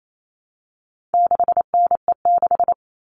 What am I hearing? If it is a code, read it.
Morse code, heard as a keyed tone, about 28 wpm, 700 Hz. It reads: 6NE6
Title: All Cut Numbers - 4-Digit 28wpm